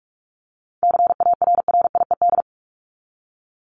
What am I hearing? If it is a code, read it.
Morse code, heard as a keyed tone, about 44 wpm, 700 Hz. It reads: CARRIED
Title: ED Suffix - Top 100 Words with Base Words 45wpm